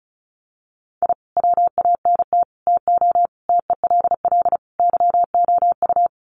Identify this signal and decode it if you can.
Morse code, heard as a keyed tone, about 34 wpm, 700 Hz. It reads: I WANT TO TELL YOU
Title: Sets of 5 Words Encore 35wpm